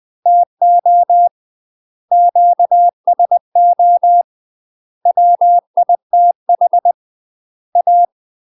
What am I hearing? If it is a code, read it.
Morse code, heard as a keyed tone, about 20 wpm, 700 Hz. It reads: TO QSO WITH A
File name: UK MASTER QSOs TRAINING / 20WPM / UK MASTER QSOs 0069